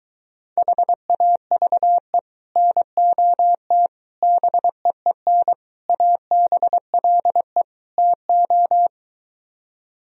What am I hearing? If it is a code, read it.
Morse code, heard as a keyed tone, about 23 wpm, 700 Hz. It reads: HAVE NOT BEEN ABLE TO